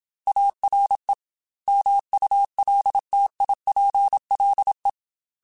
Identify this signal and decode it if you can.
Morse code, heard as a keyed tone, about 27 wpm, 800 Hz. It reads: ARE MULTIPLE
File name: Technical Subjects / Technical Subjects-mp3s / AND Gate @25